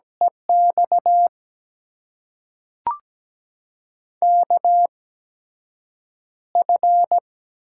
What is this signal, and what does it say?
Morse code, heard as a keyed tone, about 17 wpm, spaced as for 7 wpm, 700 Hz. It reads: EX KF
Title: USA Call Signs - 6 Characters - 8x Character Spacing - 17wpm